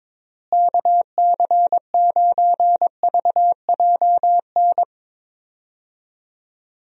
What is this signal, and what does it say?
Morse code, heard as a keyed tone, about 22 wpm, 700 Hz. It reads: KC9VJN
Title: USA Call Signs - 6 Characters - Repeated 2x - 22wpm